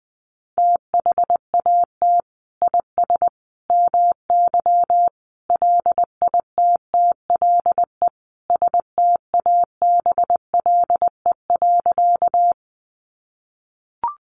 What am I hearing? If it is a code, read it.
Morse code, heard as a keyed tone, about 20 wpm, 700 Hz. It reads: THAT IS MY LITTLE STABLE.
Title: Sentences from Top 1500 Words 20wpm